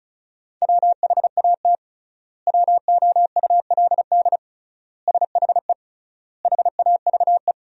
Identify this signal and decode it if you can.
Morse code, heard as a keyed tone, about 36 wpm, 700 Hz. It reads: WHAT WOULD SHE HAVE
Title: 3x Word-Spacing - Sentences from Top 100 Words - 35wpm